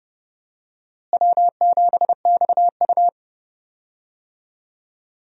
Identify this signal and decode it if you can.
Morse code, heard as a keyed tone, about 32 wpm, 700 Hz. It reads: W7XU
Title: USA Call Signs - 4 Characters - Repeated 2x - 30wpm